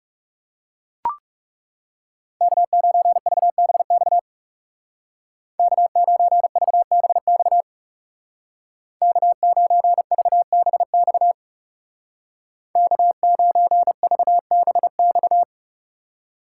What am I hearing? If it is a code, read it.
Morse code, heard as a keyed tone, about 35 wpm, 700 Hz. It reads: K9VBX K9VBX K9VBX K9VBX
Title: Speed Racing - USA 1x3 Call Signs 30wpm